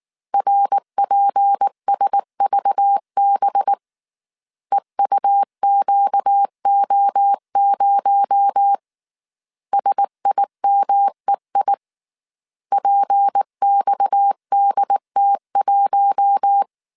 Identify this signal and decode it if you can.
Morse code, heard as a keyed tone, about 19 wpm, 800 Hz. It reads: RPSVB EUQO0 SIMEI PXDT1
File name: CW / DET / 1 6 1500C S A 18wpm 2255